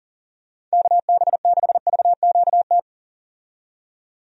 Morse code, heard as a keyed tone, about 40 wpm, 700 Hz. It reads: KB6VQT
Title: USA Call Signs - 6 Characters - 40wpm